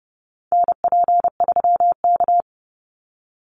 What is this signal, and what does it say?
Morse code, heard as a keyed tone, about 30 wpm, 700 Hz. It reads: NP3K
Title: Call Signs - 30wpm